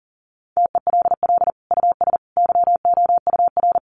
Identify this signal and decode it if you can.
Morse code, heard as a keyed tone, about 40 wpm, 700 Hz. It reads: TELL US YOUR
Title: Sentences from Top 1200 Words 40wpm